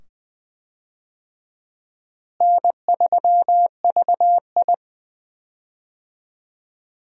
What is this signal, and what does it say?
Morse code, heard as a keyed tone, about 20 wpm, 700 Hz. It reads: N3VI
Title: USA Call Signs - 4 Characters - Repeated 2x - 20wpm